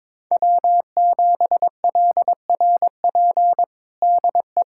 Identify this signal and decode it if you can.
Morse code, heard as a keyed tone, about 22 wpm, 700 Hz. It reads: W7LRP DE
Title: USA Callsign DE Callsign 22wpm